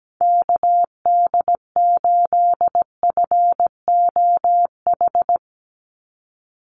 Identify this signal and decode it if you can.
Morse code, heard as a keyed tone, about 17 wpm, 700 Hz. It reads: KD8FOH
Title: USA Call Signs 17wpm